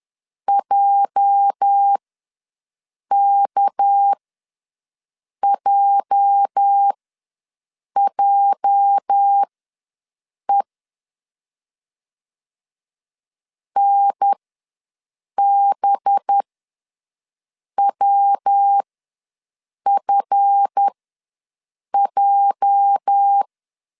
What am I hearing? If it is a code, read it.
Morse code, heard as a keyed tone, about 10 wpm, spaced as for 6 wpm, 800 Hz. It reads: JKJJE NBWFJ
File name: LEZIONI / 8 YJ45 / 8 YJ ws 10wpm 4043